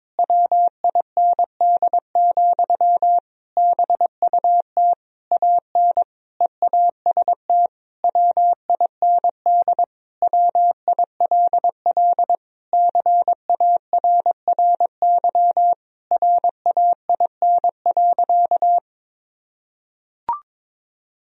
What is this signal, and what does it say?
Morse code, heard as a keyed tone, about 22 wpm, 700 Hz. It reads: WIND, BUT AN EAST WIND WILL CARRY RAIN.
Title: Sentences from Top 1000 Words - Review Part 3 - 22wpm